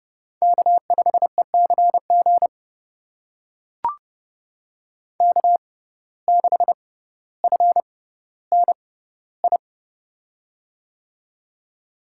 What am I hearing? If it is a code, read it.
Morse code, heard as a keyed tone, about 32 wpm, spaced as for 16 wpm, 700 Hz. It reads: K5ECG K6FNI
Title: USA Call Signs - 5 Characters - 6x Character Spacing - 30wpm